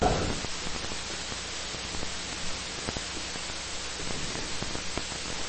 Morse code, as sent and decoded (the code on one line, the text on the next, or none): none